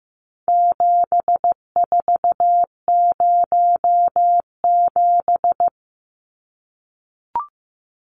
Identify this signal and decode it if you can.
Morse code, heard as a keyed tone, about 14 wpm, 700 Hz. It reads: 7407
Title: Numbers - 4-Digit 15wpm